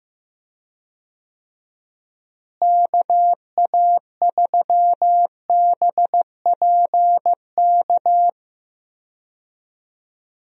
Morse code, heard as a keyed tone, about 15 wpm, 700 Hz. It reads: KA3BPK
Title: USA Call Signs - 6 Characters - Repeated 3x - 15wpm